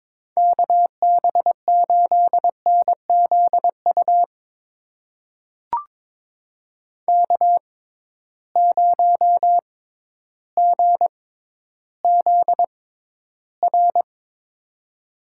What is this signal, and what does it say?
Morse code, heard as a keyed tone, about 22 wpm, spaced as for 12 wpm, 700 Hz. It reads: KB8NZU K0GZR
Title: USA Call Signs - 6x Character Spacing - 22wpm